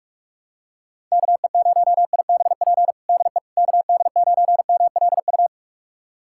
Morse code, heard as a keyed tone, about 46 wpm, 700 Hz. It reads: KE0IBP DE KD9MLU
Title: USA Callsign DE Callsign - 6 Characters - 45wpm